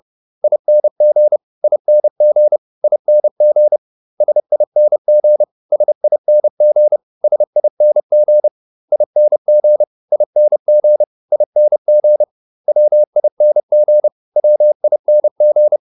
Morse code, heard as a keyed tone, about 30 wpm, 600 Hz. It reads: ING ING ING SING SING SING ING ING ING WING WING